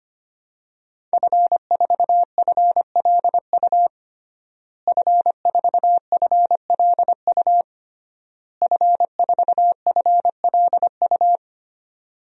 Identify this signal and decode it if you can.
Morse code, heard as a keyed tone, about 25 wpm, 700 Hz. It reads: F4FLU F4FLU F4FLU
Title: DXCC-100 Call Signs - 5 Characters - Repeated 3x - 25wpm